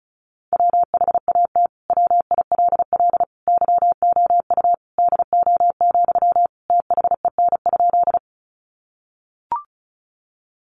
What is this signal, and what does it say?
Morse code, heard as a keyed tone, about 34 wpm, 700 Hz. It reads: WHAT WILL YOU DO, THEN?
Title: Sentences from Top 100 Words 35wpm